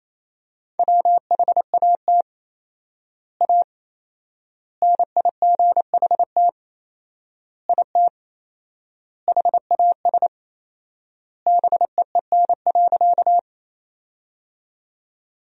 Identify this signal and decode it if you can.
Morse code, heard as a keyed tone, about 27 wpm, 700 Hz. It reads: WHAT A NIGHT IT HAS BEEN.
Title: 4x Word-Spacing - Sentences from Top 300 Words - 28wpm